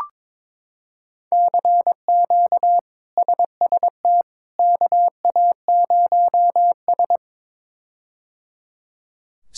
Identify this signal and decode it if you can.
Morse code, heard as a keyed tone, about 22 wpm, 700 Hz. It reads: CQ SST KA0S
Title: K1USN SST QSO Practice - 22wpm